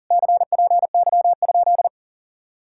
Morse code, heard as a keyed tone, about 47 wpm, 700 Hz. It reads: CPY?
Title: Speed-Racing Common QSO Phrases - 40wpm